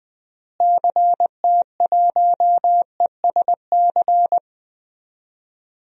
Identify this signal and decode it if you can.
Morse code, heard as a keyed tone, about 20 wpm, 700 Hz. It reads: CT1ESC
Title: DXCC-100 Call Signs - 6 Characters - Repeated 2x - 20wpm